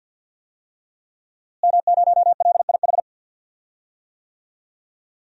Morse code, heard as a keyed tone, about 52 wpm, 700 Hz. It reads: M0LIH
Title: DXCC-100 Call Signs - 5 Characters - Repeated 2x - 50wpm